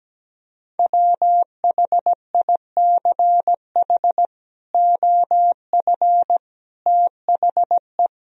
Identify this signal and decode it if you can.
Morse code, heard as a keyed tone, about 17 wpm, 700 Hz. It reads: WHICH OF THE
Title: Sentences from Top 1000 Words - Review Part 1 - 17wpm